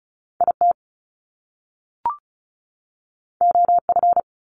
Morse code, heard as a keyed tone, about 36 wpm, 700 Hz. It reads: IT OF
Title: Top Two Letter Combinations in Words 35wpm